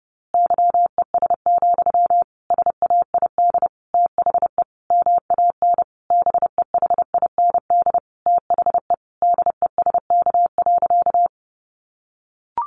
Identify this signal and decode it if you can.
Morse code, heard as a keyed tone, about 30 wpm, 700 Hz. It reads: YES, SAID THE MAN BEHIND THE DESK.
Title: A Sound Of Thunder 30wpm